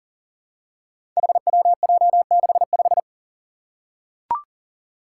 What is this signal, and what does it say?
Morse code, heard as a keyed tone, about 38 wpm, 700 Hz. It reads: HWJ65
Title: Five Challenging Letter-Number Combinations 40wpm